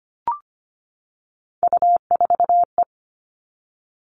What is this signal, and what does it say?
Morse code, heard as a keyed tone, about 25 wpm, 700 Hz. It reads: U4E
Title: All Cut Numbers - 3-Digit 25wpm